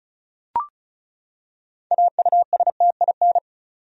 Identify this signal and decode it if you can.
Morse code, heard as a keyed tone, about 34 wpm, 700 Hz. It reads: AUSTIN
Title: USA Common Names - 35wpm